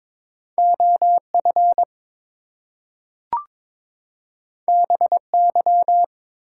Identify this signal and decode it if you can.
Morse code, heard as a keyed tone, about 22 wpm, 700 Hz. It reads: OF BY
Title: Two Letter Words - 22wpm